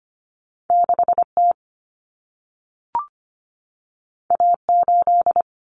Morse code, heard as a keyed tone, about 25 wpm, 700 Hz. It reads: 6T A8
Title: Common Cut Numbers - 2-Digit 25wpm